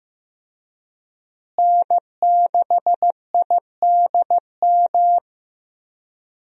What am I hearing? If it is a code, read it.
Morse code, heard as a keyed tone, about 15 wpm, 700 Hz. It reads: N6IDM
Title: USA Call Signs - 5 Characters - Repeated 3x - 15wpm